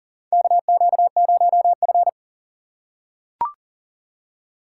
Morse code, heard as a keyed tone, about 40 wpm, 700 Hz. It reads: KQ0F
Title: USA Call Signs - 4 Characters - 3x Character Spacing - 40wpm